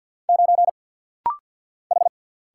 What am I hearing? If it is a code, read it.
Morse code, heard as a keyed tone, about 50 wpm, 700 Hz. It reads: 9 H
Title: Single Letter-Number Alt - Rapid-Fire 50wpm